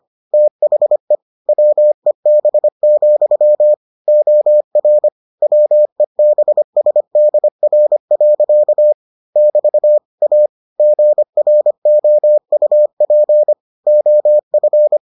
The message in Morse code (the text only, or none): THE WEB, OR WEBSDR. <BT> A GROUP OF